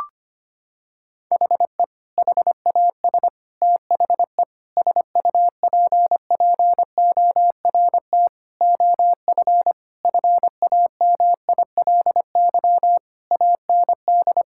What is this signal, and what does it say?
Morse code, heard as a keyed tone, about 25 wpm, 700 Hz. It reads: HE HAS THE SUPPORT OF FAMILY AND
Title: Sentences from Top 1000 Words - Review Part 5 - 25wpm